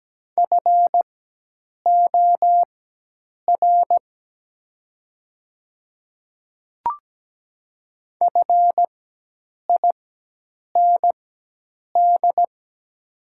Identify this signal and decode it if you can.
Morse code, heard as a keyed tone, about 17 wpm, spaced as for 8 wpm, 700 Hz. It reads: FOR FIND
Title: 4x Character Spacing - Top 100 Words - 17wpm